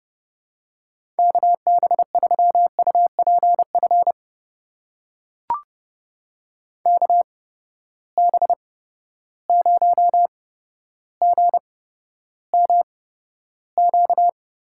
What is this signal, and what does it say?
Morse code, heard as a keyed tone, about 30 wpm, spaced as for 13 wpm, 700 Hz. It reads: KB3UPF KB0GMQ
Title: USA Call Signs - 6 Characters - 8x Character Spacing - 30wpm